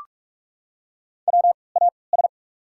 Morse code, heard as a keyed tone, about 43 wpm, spaced as for 25 wpm, 690 Hz. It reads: WAS
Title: 3x Character Spacing - Top 300 Words - 45wpm